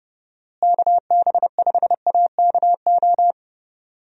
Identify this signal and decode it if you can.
Morse code, heard as a keyed tone, about 32 wpm, 700 Hz. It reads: KB5AKO
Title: USA Call Signs - 6 Characters - Repeated 2x - 30wpm